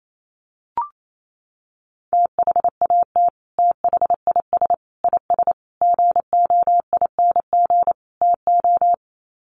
Morse code, heard as a keyed tone, about 28 wpm, 700 Hz. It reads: THAT THIS IS GOING TO
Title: Sets of 5 Words Encore2 28wpm